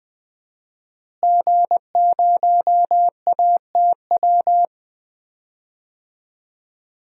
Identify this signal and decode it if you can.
Morse code, heard as a keyed tone, about 20 wpm, 700 Hz. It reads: G0ATW
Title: DXCC-100 Call Signs - 5 Characters - Repeated 2x - 20wpm